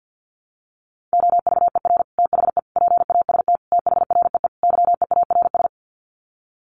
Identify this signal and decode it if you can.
Morse code, heard as a keyed tone, about 48 wpm, 700 Hz. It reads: OVER THE PAST THREE YEARS
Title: Sets of 5 Words Encore2 50wpm